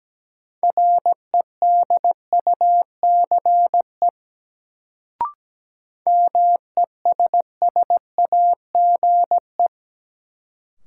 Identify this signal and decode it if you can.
Morse code, heard as a keyed tone, about 17 wpm, 700 Hz. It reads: REDUCE MESSAGE
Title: Sentences from Top 1100 Words - New Word Drill - 17wpm